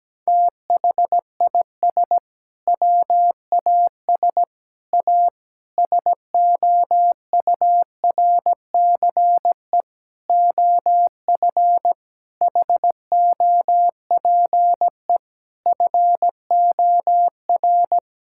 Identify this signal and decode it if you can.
Morse code, heard as a keyed tone, about 17 wpm, 700 Hz. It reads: THIS WAS A SOURCE OF HOPE FOR
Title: Sentences from Top 1000 Words - Review Part 5 - 17wpm